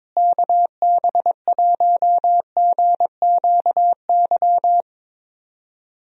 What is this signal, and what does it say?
Morse code, heard as a keyed tone, about 22 wpm, 700 Hz. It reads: KB1GQY
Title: USA Call Signs - 6 Characters - Repeated 3x - 22wpm